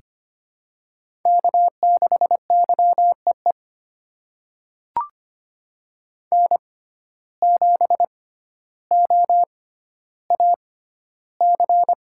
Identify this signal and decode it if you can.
Morse code, heard as a keyed tone, about 25 wpm, spaced as for 13 wpm, 700 Hz. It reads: K6YEE N7OAC
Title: USA Call Signs - 5 Characters - 6x Character Spacing - 25wpm